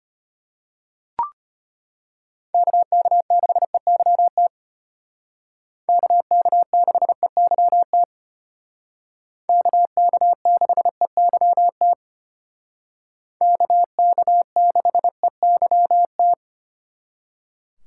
Morse code, heard as a keyed tone, about 31 wpm, 700 Hz. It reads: KK6EYT KK6EYT KK6EYT KK6EYT
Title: Speed Racing - USA 2x3 Call Signs 25wpm